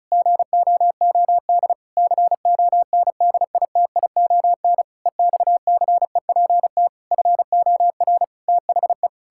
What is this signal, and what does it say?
Morse code, heard as a keyed tone, about 34 wpm, 700 Hz. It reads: GOOD CONDITION EXCEPT FOR THE
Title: Pangram Challenge v2 - 35wpm